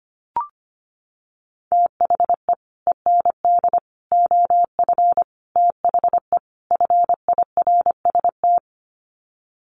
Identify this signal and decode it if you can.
Morse code, heard as a keyed tone, about 25 wpm, 700 Hz. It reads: THE END OF THE FIRST
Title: Sets of 5 Words Encore 25wpm